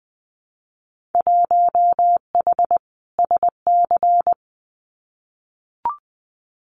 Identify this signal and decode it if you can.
Morse code, heard as a keyed tone, about 20 wpm, 700 Hz. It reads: 1H SC
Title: Winter Field Day Exchanges 20wpm